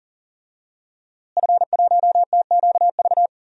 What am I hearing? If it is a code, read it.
Morse code, heard as a keyed tone, about 40 wpm, 700 Hz. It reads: F1TQV
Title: DXCC-100 Call Signs - 5 Characters - Repeated 2x - 40wpm